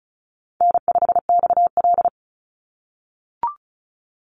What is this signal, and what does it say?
Morse code, heard as a keyed tone, about 34 wpm, 700 Hz. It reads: N5XL
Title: NA SOTA Chaser Callsigns 35wpm